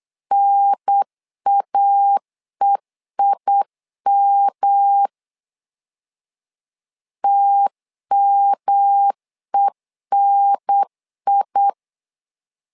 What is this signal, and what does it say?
Morse code, heard as a keyed tone, about 8 wpm, 800 Hz. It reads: NAEIM TMENI
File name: LEZIONI / 3 MWRO / 3 M 8wpm 2300